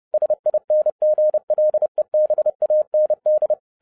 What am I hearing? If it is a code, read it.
Morse code, heard as a keyed tone, about 30 wpm, 610 Hz. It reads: SINGLEBAND
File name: 30wpm